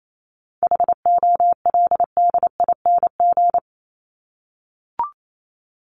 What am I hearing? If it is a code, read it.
Morse code, heard as a keyed tone, about 28 wpm, 700 Hz. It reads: HOLDING
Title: ING Suffix - Top 100 Words with Base Words 28wpm